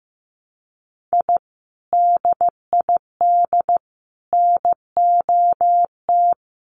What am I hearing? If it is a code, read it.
Morse code, heard as a keyed tone, about 15 wpm, 700 Hz. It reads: I DID NOT
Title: Sets of 5 Words Encore2 15wpm